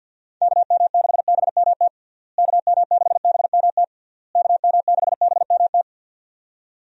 Morse code, heard as a keyed tone, about 52 wpm, 700 Hz. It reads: KM6BMT KM6BMT KM6BMT
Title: USA Call Signs - 6 Characters - Repeated 3x - 50wpm